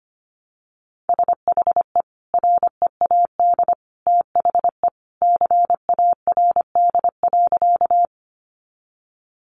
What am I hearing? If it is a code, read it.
Morse code, heard as a keyed tone, about 25 wpm, 700 Hz. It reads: SHE READ THE CARD.
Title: Sentences from Top 600 Words 25wpm